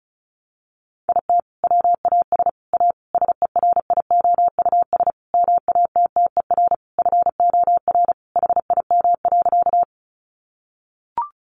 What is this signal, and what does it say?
Morse code, heard as a keyed tone, about 34 wpm, 700 Hz. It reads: IT WAS A SERIOUS MATTER FOR HIM.
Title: Sentences from Top 1100 Words 35wpm